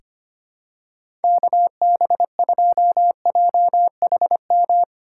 Morse code, heard as a keyed tone, about 25 wpm, 700 Hz. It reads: KB2JHM
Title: USA Call Signs - 6 Characters - 3x Character Spacing - 25wpm